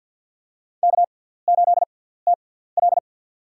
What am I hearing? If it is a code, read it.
Morse code, heard as a keyed tone, about 53 wpm, spaced as for 17 wpm, 700 Hz. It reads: K8TL
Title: USA Call Signs - 4 Characters - 6x Character Spacing - 50wpm